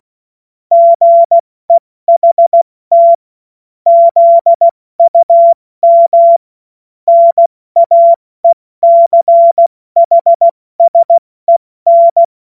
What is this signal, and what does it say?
Morse code, heard as a keyed tone, about 16 wpm, 680 Hz. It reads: GEHT ZUM NAECHSEN
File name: Morsekurs / Witze / Witz 14. Juni 2022 70 F80